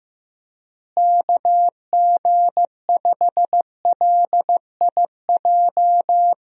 Morse code, heard as a keyed tone, about 15 wpm, 700 Hz. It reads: KG5LIJ
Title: USA Call Signs - 6 Characters - Repeated 3x - 15wpm